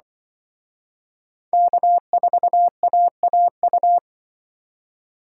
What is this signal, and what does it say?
Morse code, heard as a keyed tone, about 23 wpm, 700 Hz. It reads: K4AAU
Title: Speed Racing - USA 1x3 Call Signs 20wpm